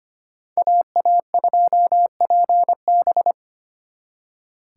Morse code, heard as a keyed tone, about 30 wpm, 700 Hz. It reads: AA2PB
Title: Speed Racing - USA 2x2 Call Signs 25wpm